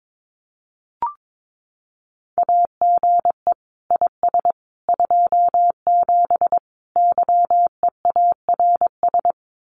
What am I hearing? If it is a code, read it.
Morse code, heard as a keyed tone, about 22 wpm, 700 Hz. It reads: AGE IS 27 YEARS
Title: Common QSO Phrases Encore 22wpm